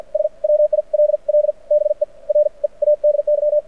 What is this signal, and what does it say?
Morse code, heard as a keyed tone, about 52 wpm, spaced as for 41 wpm, 590 Hz. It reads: S WILL BE READY
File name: archived / CRSnet